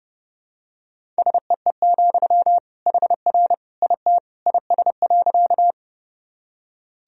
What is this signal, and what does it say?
Morse code, heard as a keyed tone, about 30 wpm, 700 Hz. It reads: SEE, HR IT IS.
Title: Sentences from Top 200 Words and Abbreviations 30wpm